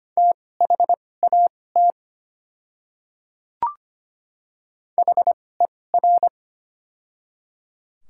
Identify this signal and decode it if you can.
Morse code, heard as a keyed tone, about 25 wpm, spaced as for 18 wpm, 690 Hz. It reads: THAT HER
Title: 2x Character Spacing - Top 100 Words - 25wpm